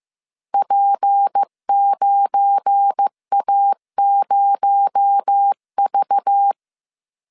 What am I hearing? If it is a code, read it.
Morse code, heard as a keyed tone, about 15 wpm, 800 Hz. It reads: P9A0V
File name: CW / DET / 1 9 1500C S C 14wpm 3245